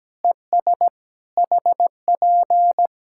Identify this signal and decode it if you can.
Morse code, heard as a keyed tone, about 17 wpm, 700 Hz. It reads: ES HP
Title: Practice QSO - Exchange 4 - Ending - 17wpm